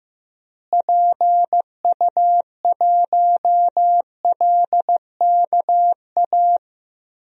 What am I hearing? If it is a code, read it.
Morse code, heard as a keyed tone, about 15 wpm, 700 Hz. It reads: PU1LKA
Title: DXCC-100 Call Signs - 6 Characters - Repeated 3x - 15wpm